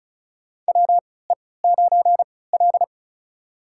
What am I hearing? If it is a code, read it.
Morse code, heard as a keyed tone, about 36 wpm, spaced as for 20 wpm, 700 Hz. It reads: WE9L
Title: USA Call Signs - 3x Character Spacing - 35wpm